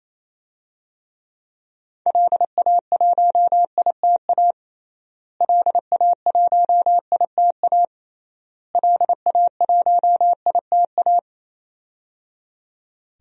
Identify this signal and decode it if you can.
Morse code, heard as a keyed tone, about 28 wpm, 700 Hz. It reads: LA1ITA LA1ITA LA1ITA
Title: DXCC-100 Call Signs - 6 Characters - Repeated 3x - 28wpm